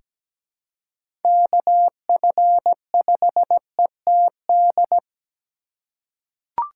Morse code, heard as a keyed tone, about 17 wpm, 700 Hz. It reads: KF5ETD
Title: K1USN SST QSO Practice - 17wpm